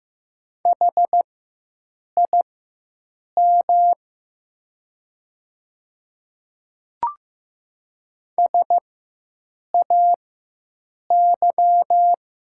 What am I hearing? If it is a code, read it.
Morse code, heard as a keyed tone, about 15 wpm, spaced as for 7 wpm, 700 Hz. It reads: HIM SAY
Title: 4x Character Spacing - Top 200 Words - 15wpm